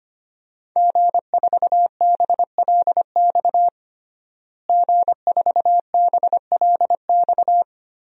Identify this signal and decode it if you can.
Morse code, heard as a keyed tone, about 25 wpm, 700 Hz. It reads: G4BLX G4BLX
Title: DXCC-100 Call Signs - 5 Characters - Repeated 2x - 25wpm